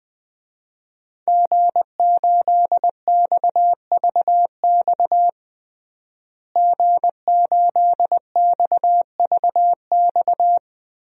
Morse code, heard as a keyed tone, about 20 wpm, 700 Hz. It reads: G8XVX G8XVX
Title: DXCC-100 Call Signs - 5 Characters - Repeated 2x - 20wpm